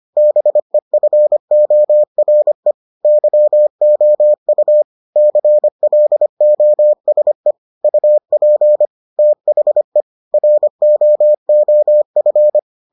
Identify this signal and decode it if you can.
Morse code, heard as a keyed tone, about 25 wpm, 600 Hz. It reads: BEFORE YOU CLOSE UP THE ROOF